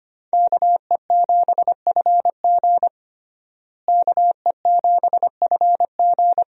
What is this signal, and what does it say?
Morse code, heard as a keyed tone, about 25 wpm, 700 Hz. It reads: KE7FG KE7FG
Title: USA Call Signs - 5 Characters - Repeated 2x - 25wpm